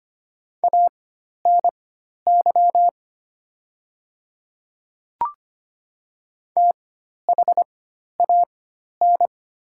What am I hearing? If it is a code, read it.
Morse code, heard as a keyed tone, about 25 wpm, spaced as for 12 wpm, 690 Hz. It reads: ANY THAN
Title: 4x Character Spacing - Top 200 Words - 25wpm